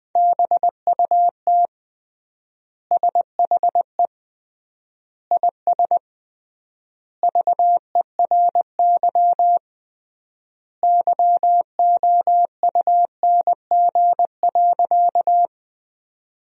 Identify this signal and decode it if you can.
Morse code, heard as a keyed tone, about 20 wpm, 700 Hz. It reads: BUT SHE IS VERY YOUNG.
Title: 3x Word-Spacing - Sentences from Top 400 Words - 20wpm